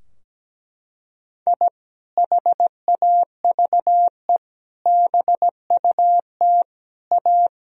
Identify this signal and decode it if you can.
Morse code, heard as a keyed tone, about 17 wpm, 700 Hz. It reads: I HAVE BUT A
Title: Sentences from Top 1000 Words - Review Part 4 - 17wpm